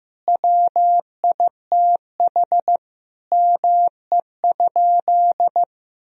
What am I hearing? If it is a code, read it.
Morse code, heard as a keyed tone, about 15 wpm, 700 Hz. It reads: WITH ME?
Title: Sentences from Top 1000 Words - Review Part 2 - 15wpm